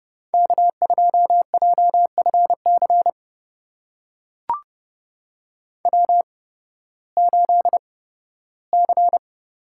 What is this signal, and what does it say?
Morse code, heard as a keyed tone, about 30 wpm, spaced as for 16 wpm, 700 Hz. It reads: K2JFC W8C
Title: USA Call Signs - 5 Characters - 8x Character Spacing - 30wpm